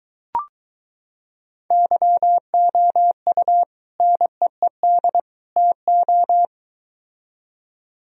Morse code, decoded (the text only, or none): YOU NEED TO